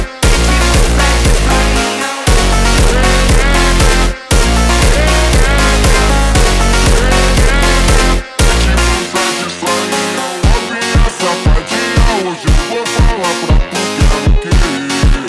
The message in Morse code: none